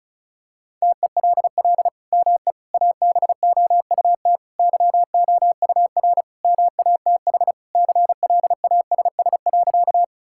Morse code, heard as a keyed tone, about 36 wpm, 700 Hz. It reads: TELL ME ABOUT YOUR MATH CLASS.
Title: Sentences from Top 1000 Words - Review Part 6 - 35wpm